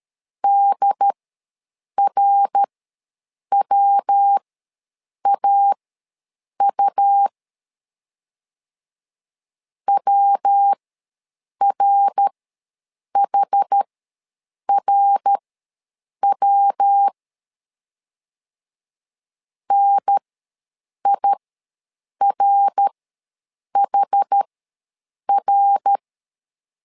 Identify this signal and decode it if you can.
Morse code, heard as a keyed tone, about 13 wpm, spaced as for 7 wpm, 800 Hz. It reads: DRWAU WRHRW NIRHR
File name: LEZIONI / 3 MWRO / 3 MWR ws 12wpm 1809